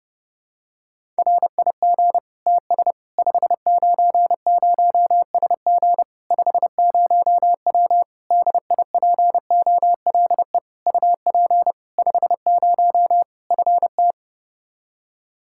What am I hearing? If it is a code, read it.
Morse code, heard as a keyed tone, about 30 wpm, 700 Hz. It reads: RIG TS 590SG 50W DIPOLE UP 50 FT
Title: Practice QSO - Exchange 2 - Info - 30wpm